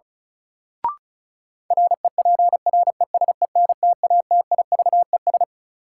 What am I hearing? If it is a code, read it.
Morse code, heard as a keyed tone, about 36 wpm, 700 Hz. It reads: REPRESENTATIVES
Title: Sets of 5 Words Encore - New Word Drill - 35wpm